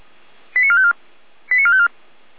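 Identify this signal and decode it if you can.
Five-tone selective call (ZVEI), heard as quick runs of five tones, tones of about 70 ms. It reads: 89454 89454
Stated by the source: ZVEI five-tone selective call received with an SDR.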